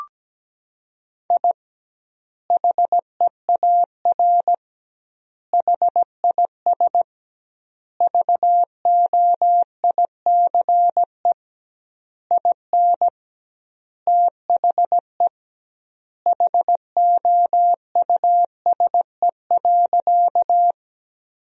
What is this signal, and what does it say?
Morse code, heard as a keyed tone, about 17 wpm, 700 Hz. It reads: I HEAR HIS VOICE IN THE HOUSE.
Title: 2x Word-Spacing - Sentences from Top 500 Words - 17wpm